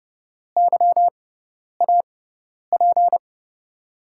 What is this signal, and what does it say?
Morse code, heard as a keyed tone, about 31 wpm, spaced as for 10 wpm, 700 Hz. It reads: YAP
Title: USA Call Signs - 5 Characters - 6x Character Spacing - 30wpm